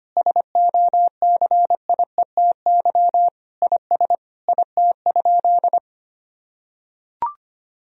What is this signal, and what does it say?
Morse code, heard as a keyed tone, about 25 wpm, 700 Hz. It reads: SOCIETY IS IT?
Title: Sentences from Top 1000 Words - Review Part 4 - 25wpm